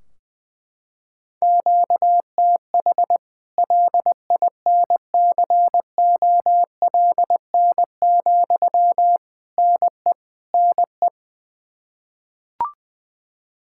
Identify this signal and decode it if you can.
Morse code, heard as a keyed tone, about 20 wpm, 700 Hz. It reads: QTH LINCOLN, NE NE
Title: Practice QSO - Exchange 1 - Beginning - 20wpm